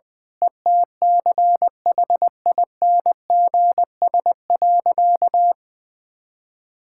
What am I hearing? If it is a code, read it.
Morse code, heard as a keyed tone, about 20 wpm, 700 Hz. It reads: ETCHINGS.
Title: Pangram Challenge v2 - 20wpm